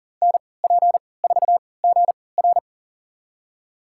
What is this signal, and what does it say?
Morse code, heard as a keyed tone, about 40 wpm, spaced as for 23 wpm, 700 Hz. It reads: NP4GR
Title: USA Call Signs - 5 Characters - 3x Character Spacing - 40wpm